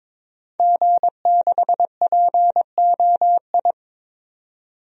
Morse code, heard as a keyed tone, about 21 wpm, 700 Hz. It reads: G6POI
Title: DXCC-100 Call Signs - 5 Characters - 22wpm